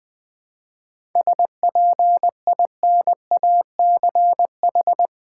Morse code, heard as a keyed tone, about 20 wpm, 700 Hz. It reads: SPINACH
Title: Difficult Words with Hints - 20wpm